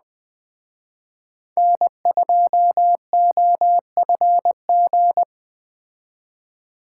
Morse code, heard as a keyed tone, about 20 wpm, 700 Hz. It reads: N2OFG
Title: Speed Racing - USA Call Signs 17wpm